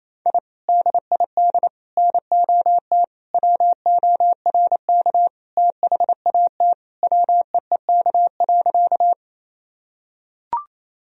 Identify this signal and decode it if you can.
Morse code, heard as a keyed tone, about 28 wpm, 700 Hz. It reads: I DID NOT WORK THAT WEEK.
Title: Sentences from Top 1000 Words - Review Part 2 - 28wpm